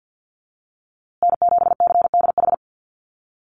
Morse code, heard as a keyed tone, about 48 wpm, 700 Hz. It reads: N7CDH
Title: USA Call Signs Repeated 3x 50wpm